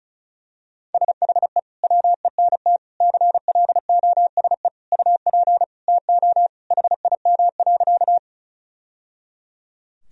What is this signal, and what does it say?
Morse code, heard as a keyed tone, about 34 wpm, 700 Hz. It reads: SHE WENT CLOSE UP TO HIM.
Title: Sentences from Top 1000 Words - Review Part 1 - 35wpm